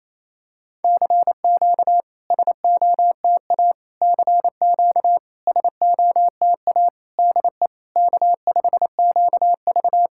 Morse code, heard as a keyed tone, about 28 wpm, 700 Hz. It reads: CQ SOTA CQ SOTA DE K5QV
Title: SOTA QSO Practice 28wpm